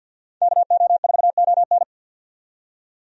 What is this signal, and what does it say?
Morse code, heard as a keyed tone, about 50 wpm, 700 Hz. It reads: KO4ON
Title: USA Call Signs - 5 Characters - 50wpm